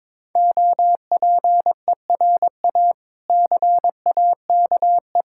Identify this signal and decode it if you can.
Morse code, heard as a keyed tone, about 22 wpm, 690 Hz. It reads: OPERA CAKE
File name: Difficult Words with Hints - 22wpm